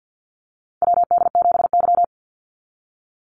Morse code, heard as a keyed tone, about 50 wpm, 700 Hz. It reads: WD7Y
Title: NA SOTA Chaser Callsigns 50wpm